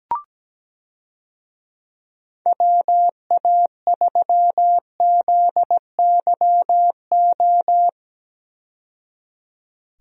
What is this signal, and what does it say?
Morse code, heard as a keyed tone, about 17 wpm, 700 Hz. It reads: WA3ZYO
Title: USA Call Signs - 6 Characters - 17wpm